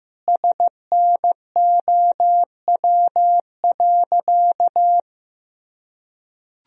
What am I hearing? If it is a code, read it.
Morse code, heard as a keyed tone, about 16 wpm, 700 Hz. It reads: SNOW.
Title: Sentences from Top 1000 Words - Review Part 3 - 15wpm